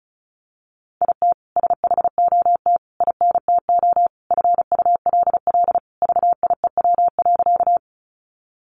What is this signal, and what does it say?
Morse code, heard as a keyed tone, about 34 wpm, 700 Hz. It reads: IT SHOT INTO FULL VIEW.
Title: Sentences from Top 1100 Words 35wpm